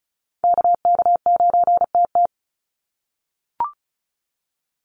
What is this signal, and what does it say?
Morse code, heard as a keyed tone, about 35 wpm, 700 Hz. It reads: KK9TT
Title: USA Call Signs 35wpm